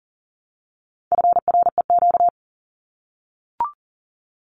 Morse code, heard as a keyed tone, about 40 wpm, 700 Hz. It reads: FREQ
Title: Abbreviations Repeated 3x - 40wpm